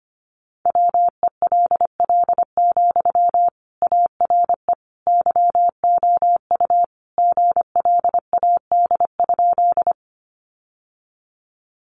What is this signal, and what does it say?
Morse code, heard as a keyed tone, about 25 wpm, 700 Hz. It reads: WELL, ARE YOU GLAD?
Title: Sentences from Top 800 Words 25wpm